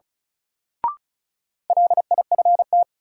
Morse code, heard as a keyed tone, about 36 wpm, 700 Hz. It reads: LIFT
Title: Sentences from Top 1000 Words - New Word Drill - 35wpm